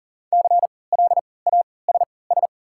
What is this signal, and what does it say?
Morse code, heard as a keyed tone, about 40 wpm, spaced as for 23 wpm, 700 Hz. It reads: CLASS
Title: 3x Character Spacing - Top 500 Words - 40wpm